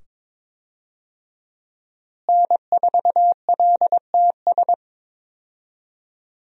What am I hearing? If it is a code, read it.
Morse code, heard as a keyed tone, about 22 wpm, 700 Hz. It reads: N4LTS
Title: USA Call Signs - 5 Characters - Repeated 2x - 22wpm